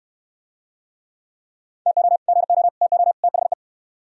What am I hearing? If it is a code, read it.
Morse code, heard as a keyed tone, about 68 wpm, 700 Hz. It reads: TO GO TO THE